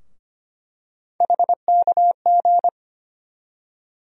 Morse code, heard as a keyed tone, about 25 wpm, 700 Hz. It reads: HKG
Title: USA Call Sign Postfixes - 3 Letters 25wpm